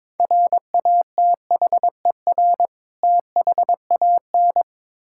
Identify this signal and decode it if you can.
Morse code, heard as a keyed tone, about 22 wpm, 700 Hz. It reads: RATHER THAN
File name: Build A Phrase - 2-Word Phrases - 22wpm